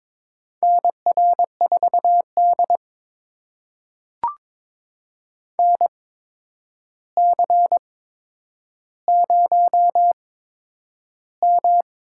Morse code, heard as a keyed tone, about 22 wpm, spaced as for 9 wpm, 700 Hz. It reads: NR4D NC0M
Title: USA Call Signs - 4 Characters - 8x Character Spacing - 22wpm